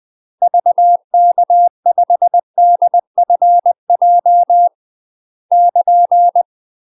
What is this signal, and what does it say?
Morse code, heard as a keyed tone, about 20 wpm, 700 Hz. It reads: VK5DFJ <KN>
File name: UK MASTER QSOs TRAINING / 20WPM / UK MASTER QSOs 0026